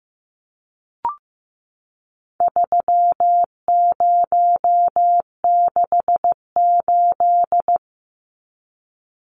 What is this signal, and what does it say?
Morse code, heard as a keyed tone, about 15 wpm, 700 Hz. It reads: 3068
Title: Numbers - 4-Digit 15wpm